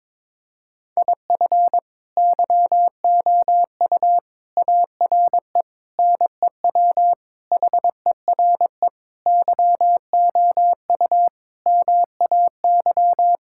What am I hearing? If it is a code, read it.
Morse code, heard as a keyed tone, about 22 wpm, 700 Hz. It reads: IF YOU ARE NEW HERE YOU MAY
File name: practice / 22wpm / Sentences with 8-Word Phrases - 22wpm